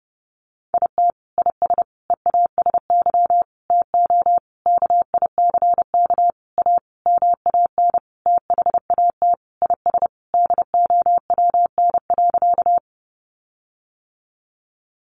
Morse code, heard as a keyed tone, about 30 wpm, 700 Hz. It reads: IT IS EASY TO KICK A MAN THAT IS DOWN.
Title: Sentences from Top 1300 Words 30wpm